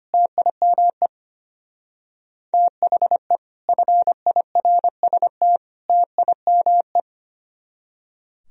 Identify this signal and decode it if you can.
Morse code, heard as a keyed tone, about 29 wpm, 700 Hz. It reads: TIME THE FIRST TIME
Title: Speed-Racing - Sets of 3 Words 25wpm